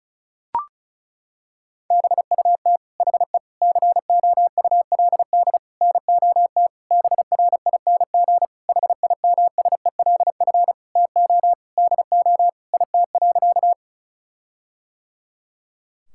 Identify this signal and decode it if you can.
Morse code, heard as a keyed tone, about 36 wpm, 700 Hz. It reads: BUT HE COULD NOT BRING HIMSELF TO DO IT.
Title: Sentences from Top 1600 Words 35wpm